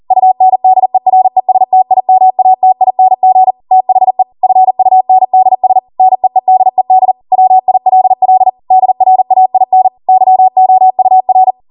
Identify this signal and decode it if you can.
Morse code, heard as a keyed tone, about 40 wpm, 750 Hz. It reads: UNDERESTIMATING THE FUNDS NEEDED WILL DRAIN YOUR